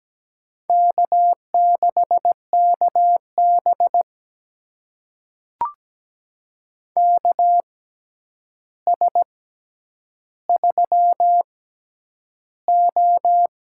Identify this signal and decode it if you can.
Morse code, heard as a keyed tone, about 17 wpm, spaced as for 9 wpm, 700 Hz. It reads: K6KB KS3O
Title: USA Call Signs - 4 Characters - 6x Character Spacing - 17wpm